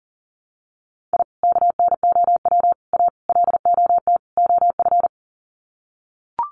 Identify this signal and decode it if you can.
Morse code, heard as a keyed tone, about 40 wpm, 700 Hz. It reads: I KNOW A LOT OF
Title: Sets of 5 Words Encore 40wpm